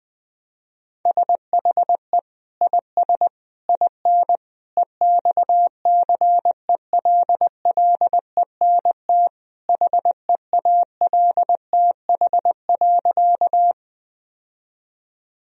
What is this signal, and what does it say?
Morse code, heard as a keyed tone, about 20 wpm, 700 Hz. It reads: SHE IS IN EXCELLENT HEALTH.
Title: Sentences from Top 1000 Words - Review Part 3 - 20wpm